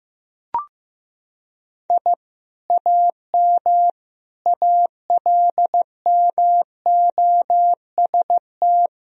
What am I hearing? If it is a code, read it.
Morse code, heard as a keyed tone, about 15 wpm, 700 Hz. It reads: I AM ALMOST
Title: Sentences from Top 1000 Words - Review Part 4 - 15wpm